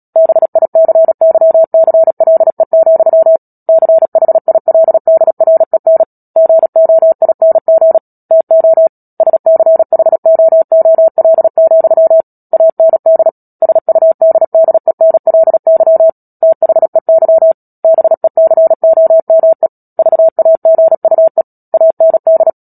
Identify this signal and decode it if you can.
Morse code, heard as a keyed tone, about 37 wpm, 650 Hz. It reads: BICYCLE, CHILDREN GOING TO SCHOOL, AND SUDDENLY THEY BECOME VAGUE AND
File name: Book Two / 10-Chapter 10 - Epilogue - 36 WPM